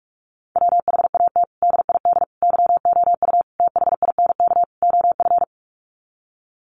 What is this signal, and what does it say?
Morse code, heard as a keyed tone, about 44 wpm, 700 Hz. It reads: WHAT DID YOU THINK OF
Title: Sets of 5 Words Encore2 45wpm